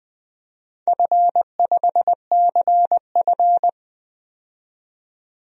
Morse code, heard as a keyed tone, about 20 wpm, 700 Hz. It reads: F5CF
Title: DXCC-100 Call Signs - 4 Characters - 20wpm